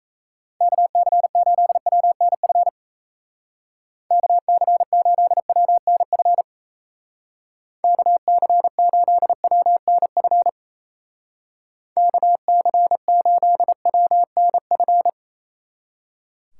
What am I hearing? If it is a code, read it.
Morse code, heard as a keyed tone, about 35 wpm, 700 Hz. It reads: KC8WNF KC8WNF KC8WNF KC8WNF